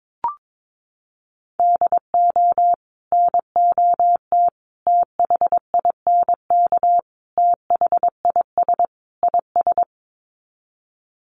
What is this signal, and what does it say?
Morse code, heard as a keyed tone, about 22 wpm, 700 Hz. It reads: DO NOT THINK THIS IS
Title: Sets of 5 Words Encore2 22wpm